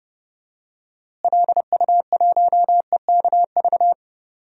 Morse code, heard as a keyed tone, about 31 wpm, 700 Hz. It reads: LU1EKV
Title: DXCC-100 Call Signs - 6 Characters - Repeated 3x - 30wpm